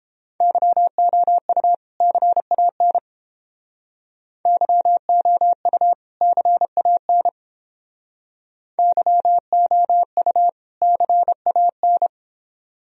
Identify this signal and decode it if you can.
Morse code, heard as a keyed tone, about 30 wpm, 700 Hz. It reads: YOU CAN YOU CAN YOU CAN